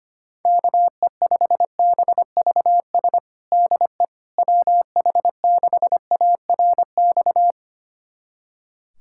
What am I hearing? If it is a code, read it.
Morse code, heard as a keyed tone, about 25 wpm, 700 Hz. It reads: KE5BVS DE WH6ARX
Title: USA Callsign DE Callsign - 6 Characters - 25wpm